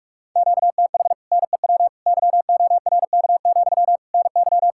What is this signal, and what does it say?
Morse code, heard as a keyed tone, about 59 wpm, spaced as for 52 wpm, 700 Hz. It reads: QTH NEW YORK, NY
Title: Speed-Racing Common QSO Phrases - 45wpm